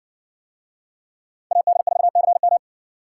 Morse code, heard as a keyed tone, about 59 wpm, 700 Hz. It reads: AD4CM